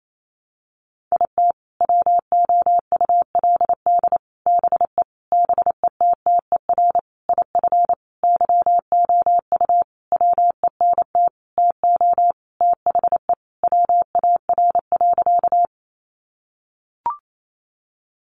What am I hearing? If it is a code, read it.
Morse code, heard as a keyed tone, about 28 wpm, 700 Hz. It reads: IT WOULD BE BETTER IF YOU WENT TO THE WAR.
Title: Sentences from Top 400 Words 28wpm